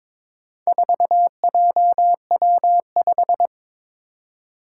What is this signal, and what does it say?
Morse code, heard as a keyed tone, about 22 wpm, 700 Hz. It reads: 4JW5
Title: Four Challenging Letter-Number Combinations 22wpm